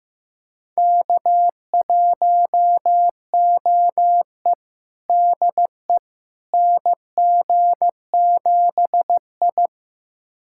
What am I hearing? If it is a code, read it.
Morse code, heard as a keyed tone, about 15 wpm, 700 Hz. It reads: K1OE DE NG7I
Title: USA Callsign DE Callsign - 4 Characters - 15wpm